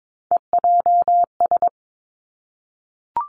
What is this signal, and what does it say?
Morse code, heard as a keyed tone, about 22 wpm, 700 Hz. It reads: EJS
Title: Three Challenging Letter-Number Combinations 22wpm